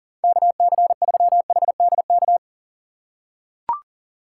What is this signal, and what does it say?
Morse code, heard as a keyed tone, about 40 wpm, 700 Hz. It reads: KC3HDK
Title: USA Call Signs - 6 Characters - 6x Character Spacing - 40wpm